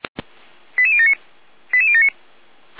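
Five-tone selective call (ZVEI), heard as quick runs of five tones, tones of about 70 ms. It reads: 88079 88079
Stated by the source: ZVEI five-tone selective call received via SDR.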